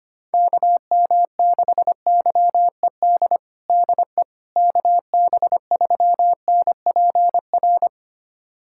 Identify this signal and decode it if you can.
Morse code, heard as a keyed tone, about 25 wpm, 700 Hz. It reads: KM6YED DE KB3NPR
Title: USA Callsign DE Callsign - 6 Characters - 25wpm